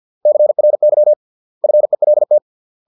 Morse code, heard as a keyed tone, about 50 wpm, 600 Hz. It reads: CRY FELT